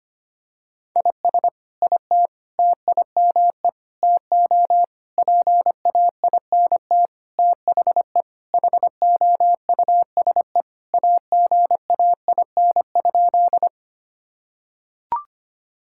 Morse code, heard as a keyed tone, about 25 wpm, 700 Hz. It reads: IS IT TIME TO PAINT THE HOUSE AGAIN?
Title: Sentences from Top 1000 Words - Review Part 6 - 25wpm